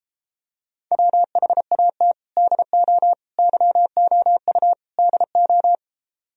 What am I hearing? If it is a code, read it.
Morse code, heard as a keyed tone, about 33 wpm, 700 Hz. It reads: WHAT DO YOU DO